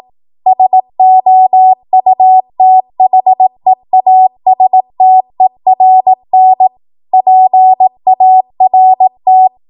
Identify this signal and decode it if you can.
Morse code, heard as a keyed tone, about 18 wpm, 750 Hz. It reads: SOUTHEASTERN PART
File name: Morse / 18WPM